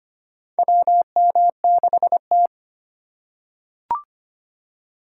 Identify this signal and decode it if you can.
Morse code, heard as a keyed tone, about 25 wpm, 700 Hz. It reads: WM6T
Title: USA Call Signs - 6x Character Spacing - 25wpm